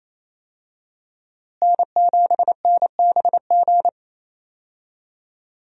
Morse code, heard as a keyed tone, about 28 wpm, 700 Hz. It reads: N7NBG
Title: USA Call Signs - 5 Characters - Repeated 2x - 28wpm